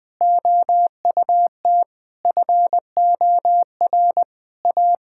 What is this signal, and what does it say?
Morse code, heard as a keyed tone, about 20 wpm, 700 Hz. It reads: OUT FOR A
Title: Difficult Sentences with Hints - 20wpm